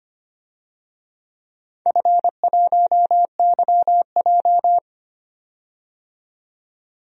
Morse code, heard as a keyed tone, about 25 wpm, 700 Hz. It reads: F1YJ
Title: DXCC-100 Call Signs - 4 Characters - 25wpm